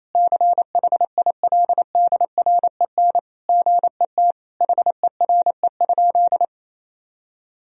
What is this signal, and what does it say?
Morse code, heard as a keyed tone, about 28 wpm, 700 Hz. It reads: CHILDREN GET HERE?
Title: Sentences from Top 1000 Words - Review Part 1 - 28wpm